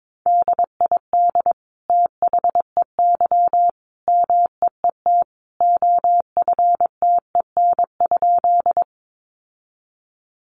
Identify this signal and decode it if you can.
Morse code, heard as a keyed tone, about 22 wpm, 700 Hz. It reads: DID THEY MEET OFTEN?
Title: Sentences from Top 1100 Words 22wpm